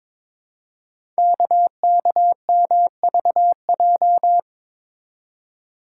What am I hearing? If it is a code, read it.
Morse code, heard as a keyed tone, about 23 wpm, 700 Hz. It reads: KKMVJ
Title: Code Groups - 5 Letters - 22wpm